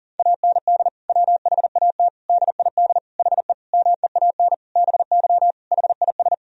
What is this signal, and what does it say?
Morse code, heard as a keyed tone, about 40 wpm, 700 Hz. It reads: AND WHAT DID HE MEAN BY HIS
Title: Sentences from Top 1000 Words - Review Part 3 - 40wpm